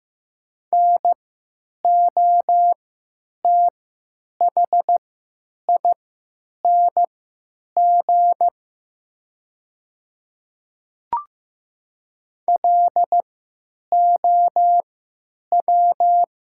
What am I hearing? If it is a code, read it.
Morse code, heard as a keyed tone, about 15 wpm, spaced as for 9 wpm, 700 Hz. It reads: NOTHING LOW
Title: 3x Character Spacing - Top 500 Words - 15wpm